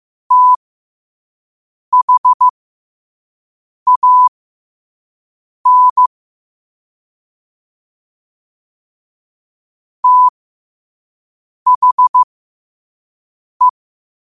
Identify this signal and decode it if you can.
Morse code, heard as a keyed tone, about 15 wpm, spaced as for 5 wpm, 1000 Hz. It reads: THAN THE
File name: qotdfiles / 5wpm